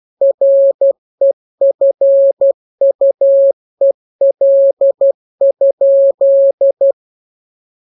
Morse code, heard as a keyed tone, about 12 wpm, 550 Hz. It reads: REFUEL?